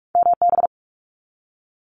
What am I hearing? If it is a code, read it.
Morse code, heard as a keyed tone, about 46 wpm, 700 Hz. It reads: MB
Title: Canadian Provinces and Territories 45wpm